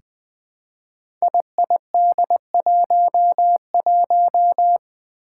Morse code, heard as a keyed tone, about 20 wpm, 700 Hz. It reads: IID11
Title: Five Challenging Letter-Number Combinations 20wpm